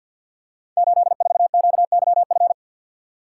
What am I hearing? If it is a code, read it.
Morse code, heard as a keyed tone, about 53 wpm, 700 Hz. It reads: 84QYF
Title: Code Groups - 5 Letter:Numbers - 50wpm